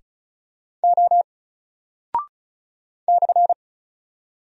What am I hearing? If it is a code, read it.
Morse code, heard as a keyed tone, about 36 wpm, 700 Hz. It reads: O /
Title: Single Character (Letter:Number:Punctuation:Prosign) 35wpm